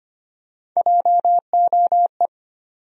Morse code, heard as a keyed tone, about 25 wpm, 700 Hz. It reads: JOE
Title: USA Common Names - Max 3 Letters - 25wpm